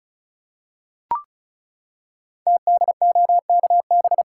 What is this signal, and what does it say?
Morse code, heard as a keyed tone, about 36 wpm, 700 Hz. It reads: TDOKB
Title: Code Groups - 5 Letters - 35wpm